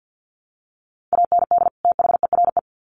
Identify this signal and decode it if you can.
Morse code, heard as a keyed tone, about 50 wpm, 700 Hz. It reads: AND THERE
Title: Sets of 2 Words 50wpm